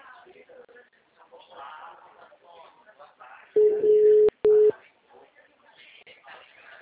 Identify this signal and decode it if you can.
Ringback tone (Europe, 425 Hz), with speech.
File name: out- -2011